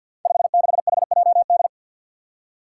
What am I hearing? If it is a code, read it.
Morse code, heard as a keyed tone, about 53 wpm, 700 Hz. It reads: 56HJD
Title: Five Challenging Letter-Number Combinations 50wpm